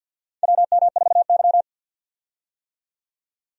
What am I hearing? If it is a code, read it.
Morse code, heard as a keyed tone, about 52 wpm, 700 Hz. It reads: WM4Y
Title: USA Call Signs - 4 Characters - 50wpm